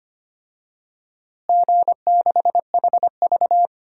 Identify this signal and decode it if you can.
Morse code, heard as a keyed tone, about 25 wpm, 700 Hz. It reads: G6HV
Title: DXCC-100 Call Signs - 4 Characters - Repeated 3x - 25wpm